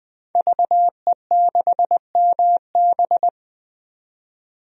Morse code, heard as a keyed tone, about 20 wpm, 700 Hz. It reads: VE6MB
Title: DXCC-100 Call Signs - 5 Characters - 20wpm